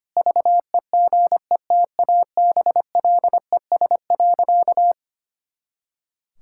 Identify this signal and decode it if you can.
Morse code, heard as a keyed tone, about 25 wpm, 700 Hz. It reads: VEGETABLES.
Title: Pangram Challenge v2 - 25wpm